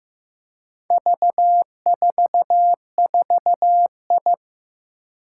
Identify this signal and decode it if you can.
Morse code, heard as a keyed tone, about 15 wpm, 700 Hz. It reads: V44I
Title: Four Challenging Letter-Number Combinations 15wpm